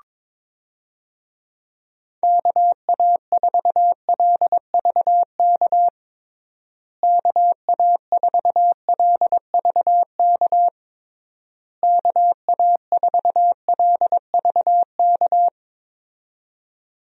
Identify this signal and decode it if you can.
Morse code, heard as a keyed tone, about 22 wpm, 700 Hz. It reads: KA4LVK KA4LVK KA4LVK
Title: USA Call Signs - 6 Characters - Repeated 3x - 22wpm